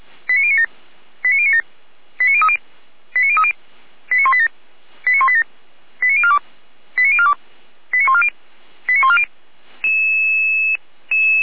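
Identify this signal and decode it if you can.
Five-tone selective call (ZVEI), heard as quick runs of five tones, tones of about 70 ms. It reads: 89097 89097 89030 89030 89187 89187 89042 89042 89140 89140
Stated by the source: Received on SDR; ZVEI five-tone selective call.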